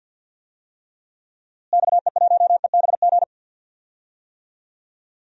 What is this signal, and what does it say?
Morse code, heard as a keyed tone, about 53 wpm, 700 Hz. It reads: XE1EBG
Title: DXCC-100 Call Signs - 6 Characters - Repeated 2x - 50wpm